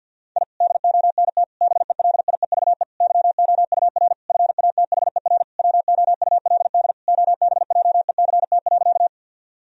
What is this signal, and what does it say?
Morse code, heard as a keyed tone, about 52 wpm, 700 Hz. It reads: I DONT BELIEVE YOUR FATHER WOULD OBJECT.
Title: Sentences from Top 1000 Words - Review Part 3 - 50wpm